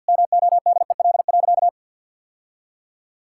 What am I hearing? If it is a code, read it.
Morse code, heard as a keyed tone, about 50 wpm, 700 Hz. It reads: MODEL.
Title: Sentences from Top 1000 Words - Review Part 5 - 50wpm